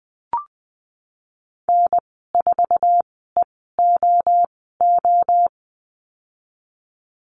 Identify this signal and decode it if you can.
Morse code, heard as a keyed tone, about 20 wpm, spaced as for 15 wpm, 700 Hz. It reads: N4EOO
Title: 048 Lesson 9g - Number 4 - Call Signs 20wpm